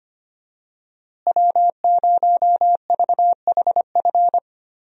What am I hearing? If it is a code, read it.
Morse code, heard as a keyed tone, about 25 wpm, 700 Hz. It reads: W0VHF
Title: USA Call Signs - 5 Characters - Repeated 3x - 25wpm